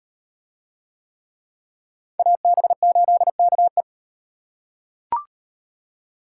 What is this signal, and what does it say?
Morse code, heard as a keyed tone, about 38 wpm, 700 Hz. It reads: AB8KE